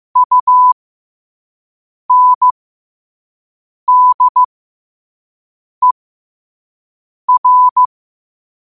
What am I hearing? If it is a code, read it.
Morse code, heard as a keyed tone, about 15 wpm, spaced as for 5 wpm, 1000 Hz. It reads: UNDER
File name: qotdfiles / 5wpm